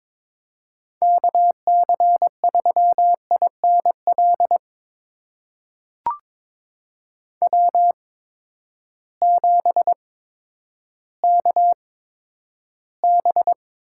Morse code, heard as a keyed tone, about 23 wpm, spaced as for 11 wpm, 700 Hz. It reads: KC3INL W7KB
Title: USA Call Signs - 8x Character Spacing - 22wpm